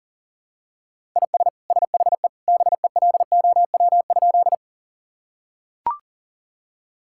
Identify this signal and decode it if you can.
Morse code, heard as a keyed tone, about 37 wpm, 700 Hz. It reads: IS SHE BELOW?
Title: Sentences from Top 1000 Words - Review Part 5 - 40wpm